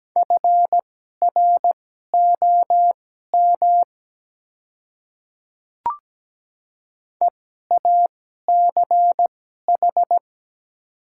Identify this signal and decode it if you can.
Morse code, heard as a keyed tone, about 17 wpm, spaced as for 12 wpm, 700 Hz. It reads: FROM EACH
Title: 2x Character Spacing - Top 100 Words - 17wpm